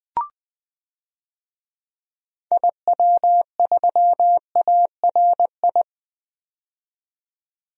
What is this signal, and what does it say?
Morse code, heard as a keyed tone, about 20 wpm, 700 Hz. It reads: IW3ARI
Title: DXCC-100 Call Signs - 6 Characters - 20wpm